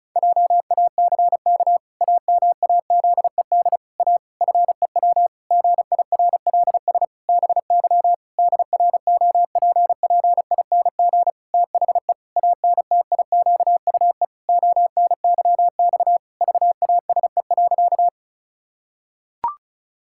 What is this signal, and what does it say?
Morse code, heard as a keyed tone, about 34 wpm, 700 Hz. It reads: JACK AMAZED A FEW GIRLS BY DROPPING THE ANTIQUE ONYX VASE.
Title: Pangram Challenge v2 - 35wpm